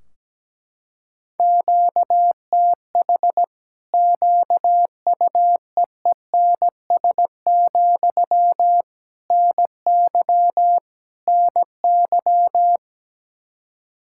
Morse code, heard as a keyed tone, about 17 wpm, 700 Hz. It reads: QTH QUEENS, NY NY
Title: Practice QSO - Exchange 1 - Beginning - 17wpm